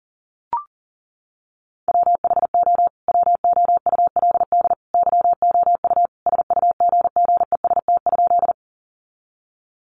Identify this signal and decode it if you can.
Morse code, heard as a keyed tone, about 37 wpm, 700 Hz. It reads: WHO WOULD YOU SUGGEST?
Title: Sentences from Top 1300 Words 40wpm